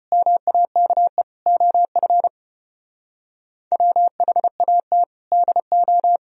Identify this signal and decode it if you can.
Morse code, heard as a keyed tone, about 33 wpm, 700 Hz. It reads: MAKE OF WHAT DO